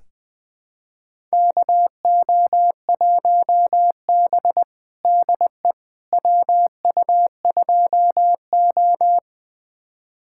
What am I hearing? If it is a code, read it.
Morse code, heard as a keyed tone, about 20 wpm, 700 Hz. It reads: KO1B DE WU2O
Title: USA Callsign DE Callsign - 4 Characters - 20wpm